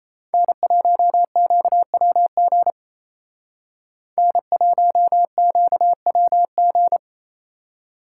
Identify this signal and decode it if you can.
Morse code, heard as a keyed tone, about 34 wpm, 700 Hz. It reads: N1QWG N1QWG
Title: Speed Racing - USA 1x3 Call Signs 28wpm